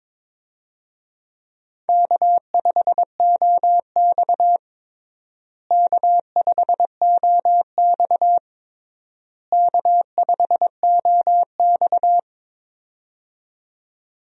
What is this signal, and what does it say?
Morse code, heard as a keyed tone, about 22 wpm, 700 Hz. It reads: K5OX K5OX K5OX
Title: USA Call Signs - 4 Characters - Repeated 3x - 22wpm